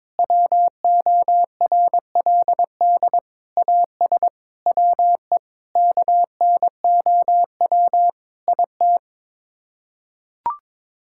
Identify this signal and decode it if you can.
Morse code, heard as a keyed tone, about 21 wpm, 700 Hz. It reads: WORLD AS WE KNOW IT
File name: practice / 22wpm / Sets of 8 Words - 22wpm